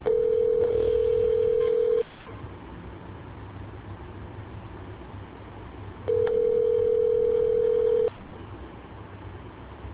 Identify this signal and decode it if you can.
Ringback tone (North America), 440 + 480 Hz.